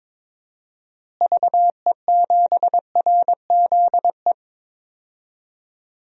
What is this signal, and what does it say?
Morse code, heard as a keyed tone, about 21 wpm, 700 Hz. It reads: VE7RZE
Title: DXCC-100 Call Signs - 6 Characters - Repeated 3x - 22wpm